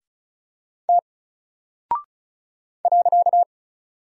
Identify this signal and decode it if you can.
Morse code, heard as a keyed tone, about 36 wpm, 700 Hz. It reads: T .
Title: Single Character (Letter:Number:Punctuation:Prosign) 35wpm